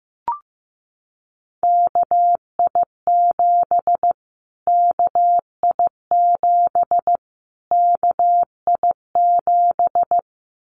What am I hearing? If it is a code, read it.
Morse code, heard as a keyed tone, about 15 wpm, 700 Hz. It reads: KI7 KI7 KI7
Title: Callsign Prefixes for Top 25 DXCC - Repeated 3x - 15wpm